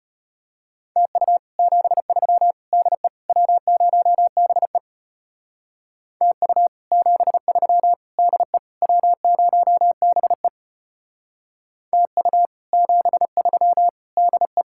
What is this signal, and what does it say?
Morse code, heard as a keyed tone, about 32 wpm, 700 Hz. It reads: TU 73 DE W0BE TU 73 DE W0BE TU 73 DE